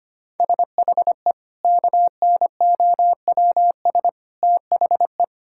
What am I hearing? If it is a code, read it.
Morse code, heard as a keyed tone, about 25 wpm, 700 Hz. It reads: SHE KNOWS THE
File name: practice / 25wpm / Sentences with 4-Word Phrases - 25wpm